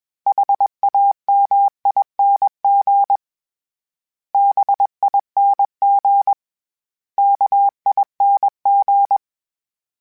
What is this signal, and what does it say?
Morse code, heard as a keyed tone, about 21 wpm, 800 Hz. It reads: HAMING BING KING
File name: Special INGs / Special INGs-mp3s / Sngl Ing @20